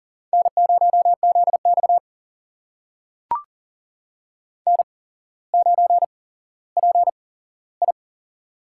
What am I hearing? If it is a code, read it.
Morse code, heard as a keyed tone, about 40 wpm, spaced as for 17 wpm, 700 Hz. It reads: N0ZX N9PI
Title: USA Call Signs - 4 Characters - 8x Character Spacing - 40wpm